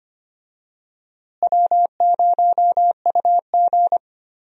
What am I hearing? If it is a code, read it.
Morse code, heard as a keyed tone, about 25 wpm, 700 Hz. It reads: W0UG
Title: USA Call Signs - 4 Characters - Repeated 3x - 25wpm